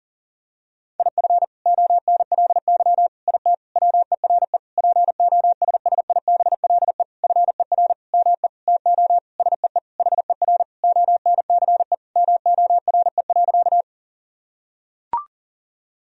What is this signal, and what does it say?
Morse code, heard as a keyed tone, about 40 wpm, 700 Hz. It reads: IF ONLY IT WERE POSSIBLE FER ME TO SEE HER ONCE MORE.
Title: Sentences from Top 500 Words and Abbreviations 40wpm